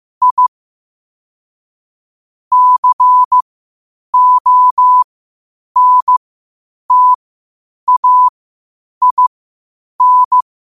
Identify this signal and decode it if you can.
Morse code, heard as a keyed tone, about 15 wpm, spaced as for 8 wpm, 1000 Hz. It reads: I CONTAIN
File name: qotdfiles / 8wpm